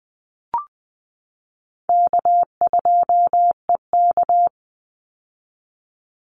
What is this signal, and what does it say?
Morse code, heard as a keyed tone, about 20 wpm, 700 Hz. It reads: K2EK
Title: Easy Call Signs - 20wpm